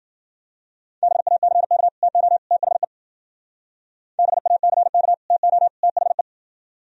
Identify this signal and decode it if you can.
Morse code, heard as a keyed tone, about 50 wpm, 700 Hz. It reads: BACK TO THE BACK TO THE
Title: Speed-Racing - Sets of 3 Words 40wpm